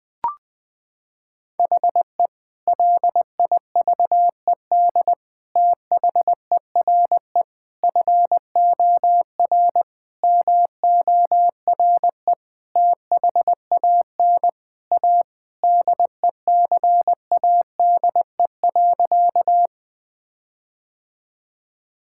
Morse code, heard as a keyed tone, about 20 wpm, 700 Hz. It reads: HE LIVED THERE FOR MORE THAN A DECADE.
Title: Sentences with 5-Word Phrases - 20wpm